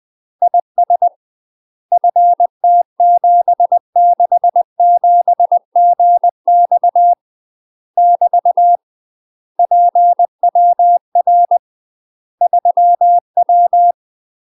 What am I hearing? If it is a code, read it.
Morse code, heard as a keyed tone, about 20 wpm, 700 Hz. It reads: IS FT767GX <BT> PWR 3W
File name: UK MASTER QSOs TRAINING / 20WPM / UK MASTER QSOs 0060